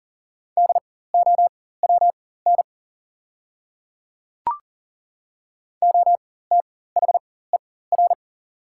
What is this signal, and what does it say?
Morse code, heard as a keyed tone, about 37 wpm, spaced as for 18 wpm, 700 Hz. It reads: DOWN OTHER
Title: 4x Character Spacing - Top 200 Words - 40wpm